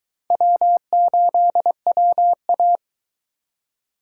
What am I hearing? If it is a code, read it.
Morse code, heard as a keyed tone, about 21 wpm, 700 Hz. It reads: W8WA